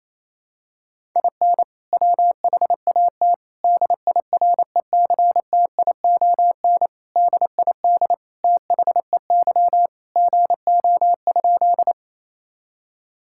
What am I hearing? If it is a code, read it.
Morse code, heard as a keyed tone, about 28 wpm, 700 Hz. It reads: IN WHAT DIRECTION DID THEY GO?
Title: Sentences from Top 1000 Words - Review Part 5 - 28wpm